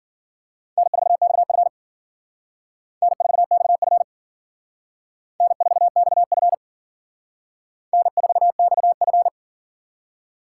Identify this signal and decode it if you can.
Morse code, heard as a keyed tone, about 52 wpm, 700 Hz. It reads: N4XF N4XF N4XF N4XF